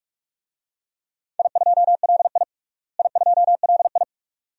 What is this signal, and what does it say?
Morse code, heard as a keyed tone, about 46 wpm, 700 Hz. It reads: I2LI I2LI
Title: DXCC-100 Call Signs - 4 Characters - Repeated 2x - 45wpm